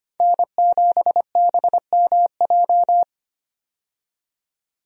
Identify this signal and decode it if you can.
Morse code, heard as a keyed tone, about 25 wpm, 700 Hz. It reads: N7BMJ
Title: USA Call Signs - 5 Characters - 25wpm